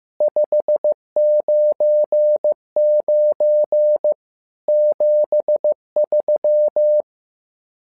Morse code, heard as a keyed tone, about 15 wpm, 600 Hz. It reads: 599 73
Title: SOTA QSO Practice 15wpm